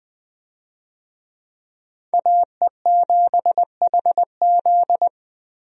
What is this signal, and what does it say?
Morse code, heard as a keyed tone, about 20 wpm, 700 Hz. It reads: AE7HZ
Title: USA Call Signs - 5 Characters - 20wpm